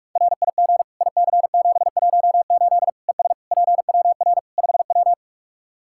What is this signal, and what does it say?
Morse code, heard as a keyed tone, about 46 wpm, 700 Hz. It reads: RIG IC718 ES PWR 5W
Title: Practice QSO - Exchange 2 - Info - 45wpm